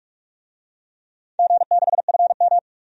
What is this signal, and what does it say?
Morse code, heard as a keyed tone, about 46 wpm, 700 Hz. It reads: G6FM
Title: DXCC-100 Call Signs - 4 Characters - 45wpm